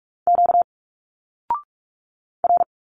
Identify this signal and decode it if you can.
Morse code, heard as a keyed tone, about 46 wpm, 700 Hz. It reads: Q R
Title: Single Letters 45wpm